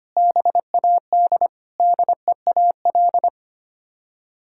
Morse code, heard as a keyed tone, about 25 wpm, 700 Hz. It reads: BAD DEAL
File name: Sets of 9 Words - 25wpm